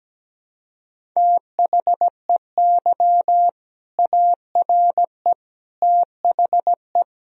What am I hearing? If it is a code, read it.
Morse code, heard as a keyed tone, about 17 wpm, 700 Hz. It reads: THEY ARE THE
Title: Sentences from Top 1000 Words - Review Part 1 - 17wpm